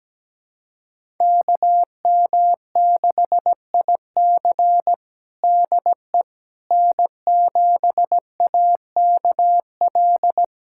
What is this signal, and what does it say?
Morse code, heard as a keyed tone, about 17 wpm, 700 Hz. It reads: KM6IC DE N7AKL
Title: USA Callsign DE Callsign - 5 Characters - 17wpm